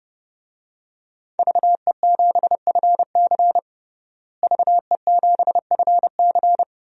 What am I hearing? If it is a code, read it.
Morse code, heard as a keyed tone, about 30 wpm, 700 Hz. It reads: VE7FC VE7FC
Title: DXCC-100 Call Signs - 5 Characters - Repeated 2x - 30wpm